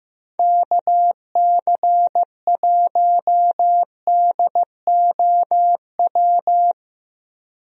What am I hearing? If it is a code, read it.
Morse code, heard as a keyed tone, about 15 wpm, 700 Hz. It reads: KC1DOW
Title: USA Call Signs - 6 Characters - 15wpm